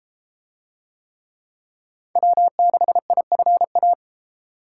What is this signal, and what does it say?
Morse code, heard as a keyed tone, about 31 wpm, 700 Hz. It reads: W6IFA